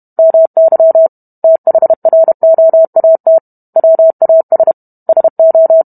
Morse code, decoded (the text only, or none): MY THROAT WAS SO